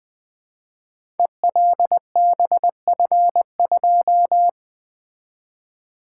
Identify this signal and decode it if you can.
Morse code, heard as a keyed tone, about 20 wpm, 700 Hz. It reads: ELBF2
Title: Code Groups - 5 Letter:Numbers - 20wpm